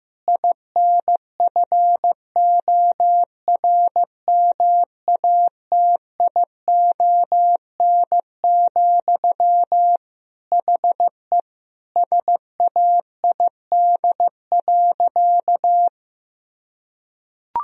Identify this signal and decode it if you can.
Morse code, heard as a keyed tone, about 15 wpm, 700 Hz. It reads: INFORMATION, HE SAID.
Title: Sentences from Top 1000 Words - Review Part 3 - 15wpm